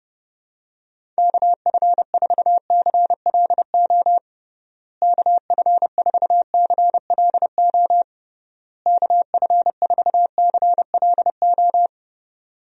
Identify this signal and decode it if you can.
Morse code, heard as a keyed tone, about 30 wpm, 700 Hz. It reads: KF4CLO KF4CLO KF4CLO
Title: USA Call Signs - 6 Characters - Repeated 3x - 30wpm